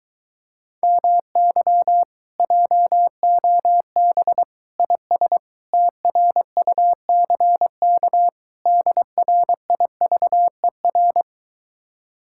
Morse code, heard as a keyed tone, about 23 wpm, 700 Hz. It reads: MY JOB IS TRUCK DRIVER